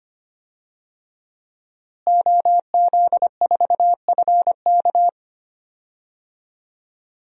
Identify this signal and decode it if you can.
Morse code, heard as a keyed tone, about 25 wpm, 700 Hz. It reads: OZ4FK
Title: DXCC-100 Call Signs - 5 Characters - Repeated 2x - 25wpm